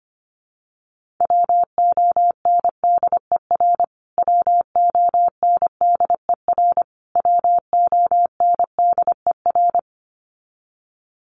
Morse code, heard as a keyed tone, about 25 wpm, 700 Hz. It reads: WONDER WONDER WONDER
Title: Top 500 Words Repeated 25wpm